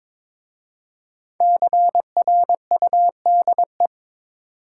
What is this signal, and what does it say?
Morse code, heard as a keyed tone, about 22 wpm, 700 Hz. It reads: CRUDE
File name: 5 Letter Words - First 3 Letters Given - 22wpm